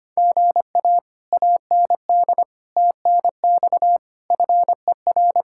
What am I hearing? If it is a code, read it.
Morse code, heard as a keyed tone, about 24 wpm, 700 Hz. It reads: GA AND TNX FER
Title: Speed-Racing Common QSO Phrases Encore - 25wpm